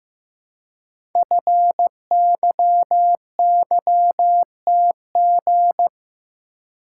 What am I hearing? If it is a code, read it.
Morse code, heard as a keyed tone, about 15 wpm, 700 Hz. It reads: FYYTG
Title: Code Groups - 5 Letters - 15wpm